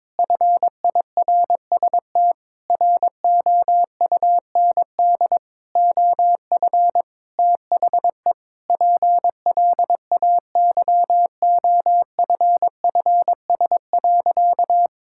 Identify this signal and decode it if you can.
Morse code, heard as a keyed tone, about 22 wpm, 700 Hz. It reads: FIRST ROUND OF THE PLAYOFFS.
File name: Sentences with 7-Word Phrases - 22wpm